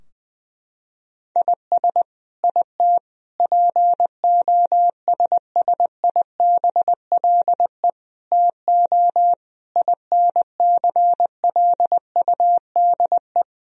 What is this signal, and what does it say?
Morse code, heard as a keyed tone, about 20 wpm, 700 Hz. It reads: IS IT POSSIBLE TO INCLUDE
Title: Sentences from Top 1000 Words - Review Part 3 - 20wpm